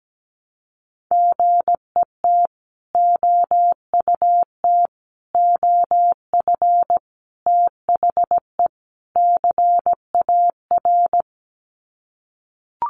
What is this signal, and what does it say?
Morse code, heard as a keyed tone, about 17 wpm, 700 Hz. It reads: GET OUT OF THE CAR
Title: Sets of 5 Words Encore2 17wpm